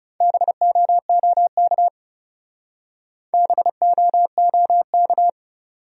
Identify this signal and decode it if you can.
Morse code, heard as a keyed tone, about 34 wpm, 700 Hz. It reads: BOOK BOOK
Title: Speed-Racing Top 300 Words 30wpm